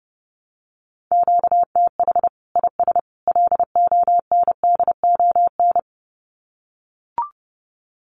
Morse code, heard as a keyed tone, about 30 wpm, 700 Hz. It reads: QTH IS LONDON
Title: Common QSO Phrases 30wpm